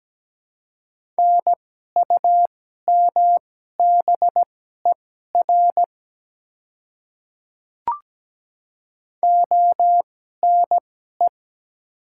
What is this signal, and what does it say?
Morse code, heard as a keyed tone, about 17 wpm, spaced as for 12 wpm, 700 Hz. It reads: NUMBER ONE
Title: 2x Character Spacing - Top 100 Words - 17wpm